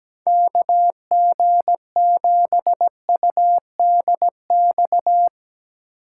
Morse code, heard as a keyed tone, about 18 wpm, 700 Hz. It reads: KG7UDX
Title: Speed Racing - USA 2x3 Call Signs 17wpm